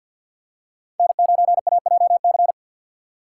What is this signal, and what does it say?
Morse code, heard as a keyed tone, about 50 wpm, 700 Hz. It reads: N9AJC
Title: USA Call Signs - 5 Characters - 2x Character Spacing - 50wpm